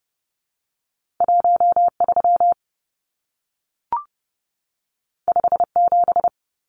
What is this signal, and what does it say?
Morse code, heard as a keyed tone, about 30 wpm, 700 Hz. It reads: 13 57
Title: Numbers - 2-Digit 30wpm